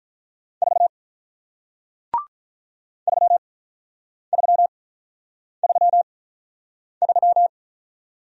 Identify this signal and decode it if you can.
Morse code, heard as a keyed tone, about 46 wpm, spaced as for 9 wpm, 700 Hz. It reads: 4 3333